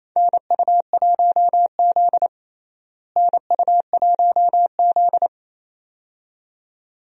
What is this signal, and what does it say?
Morse code, heard as a keyed tone, about 28 wpm, 700 Hz. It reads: NU1Z NU1Z
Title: USA Call Signs - 4 Characters - Repeated 2x - 28wpm